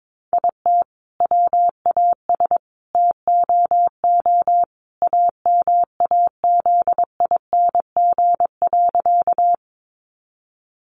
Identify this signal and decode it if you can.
Morse code, heard as a keyed tone, about 22 wpm, 700 Hz. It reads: IT WAS TOO AMAZING.
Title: Sentences from Top 1500 Words 22wpm